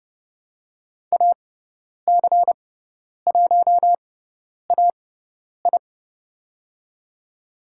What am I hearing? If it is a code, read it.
Morse code, heard as a keyed tone, about 31 wpm, spaced as for 10 wpm, 700 Hz. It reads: AC1AI
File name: Lesson 17g - Letter u - Call Signs 30wpm